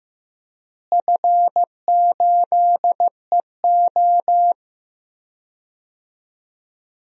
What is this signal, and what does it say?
Morse code, heard as a keyed tone, about 15 wpm, 700 Hz. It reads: F8EO
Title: DXCC-100 Call Signs - 4 Characters - Repeated 3x - 15wpm